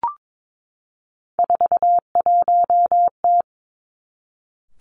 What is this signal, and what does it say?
Morse code, heard as a keyed tone, about 22 wpm, 700 Hz. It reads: 41T
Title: Three Challenging Letter-Number Combinations 22wpm